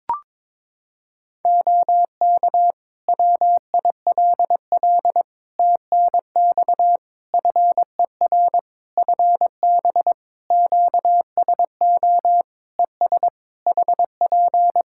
Morse code, heard as a keyed tone, about 22 wpm, 700 Hz. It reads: OK WILL TNX FER FB QSO ES HP
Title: Practice QSO - Exchange 4 - Ending - 22wpm